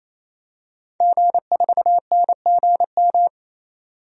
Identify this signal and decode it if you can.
Morse code, heard as a keyed tone, about 28 wpm, 700 Hz. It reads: G4NGM
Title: DXCC-100 Call Signs - 5 Characters - 28wpm